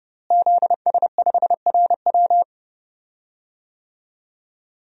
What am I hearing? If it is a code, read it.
Morse code, heard as a keyed tone, about 30 wpm, 700 Hz. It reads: ZS5RW
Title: DXCC-100 Call Signs - 5 Characters - Repeated 3x - 30wpm